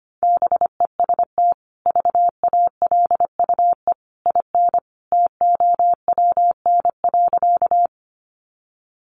Morse code, heard as a keyed tone, about 25 wpm, 700 Hz. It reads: BEST VALUE IN TOWN.
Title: Sentences from Top 600 Words 25wpm